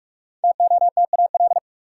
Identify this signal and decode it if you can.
Morse code, heard as a keyed tone, about 46 wpm, 700 Hz. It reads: TOTAL
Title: Five Letter Words - 45wpm